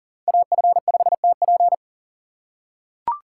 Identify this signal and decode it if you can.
Morse code, heard as a keyed tone, about 40 wpm, 690 Hz. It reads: AF5TP
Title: USA Call Signs - 5 Characters - 6x Character Spacing - 40wpm